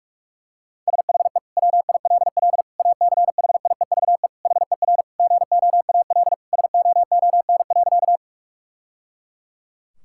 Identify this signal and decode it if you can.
Morse code, heard as a keyed tone, about 42 wpm, 700 Hz. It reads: SHE WILL ACHIEVE HER GOAL SOON.
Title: Sentences from Top 1000 Words - Review Part 6 - 45wpm